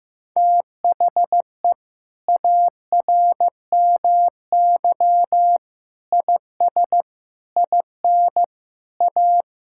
Morse code, heard as a keyed tone, about 16 wpm, 700 Hz. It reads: THE ARMY IS IN A
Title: Sentences from Top 1000 Words - Review Part 6 - 15wpm